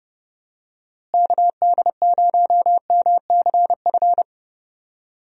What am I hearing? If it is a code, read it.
Morse code, heard as a keyed tone, about 30 wpm, 700 Hz. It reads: KD0MCF
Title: USA Call Signs - 6 Characters - Repeated 2x - 30wpm